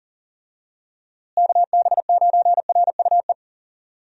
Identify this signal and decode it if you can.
Morse code, heard as a keyed tone, about 37 wpm, 700 Hz. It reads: KB9RUE
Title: USA Call Signs - 6 Characters - Repeated 3x - 40wpm